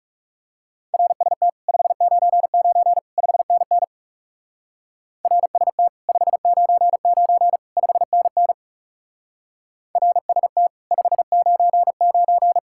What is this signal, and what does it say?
Morse code, heard as a keyed tone, about 38 wpm, 700 Hz. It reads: RST 599 5NN RST 599 5NN RST 599